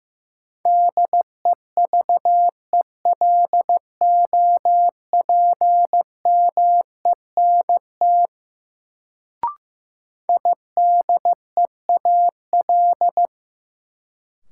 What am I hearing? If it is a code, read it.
Morse code, heard as a keyed tone, about 16 wpm, 700 Hz. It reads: DEVELOPMENT IDEAL
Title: Sentences from Top 1400 Words - New Word Drill Alt - 15wpm